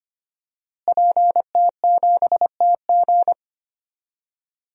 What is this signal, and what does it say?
Morse code, heard as a keyed tone, about 25 wpm, 700 Hz. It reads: PT7TG
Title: DXCC-100 Call Signs - 5 Characters - Repeated 2x - 25wpm